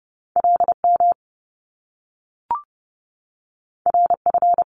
Morse code, heard as a keyed tone, about 30 wpm, 700 Hz. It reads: LM RF
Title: Top Two Letter Combinations in Words 30wpm